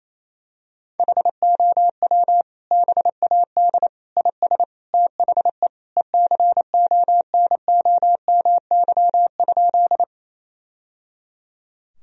Morse code, heard as a keyed tone, about 27 wpm, 700 Hz. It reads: HOW BAD IS THE ECONOMY?
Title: Sentences from Top 1000 Words - Review Part 3 - 28wpm